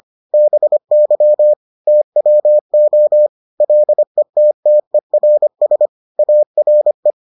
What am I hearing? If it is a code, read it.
Morse code, heard as a keyed tone, about 25 wpm, 600 Hz. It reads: BY TWO LETTERS ARE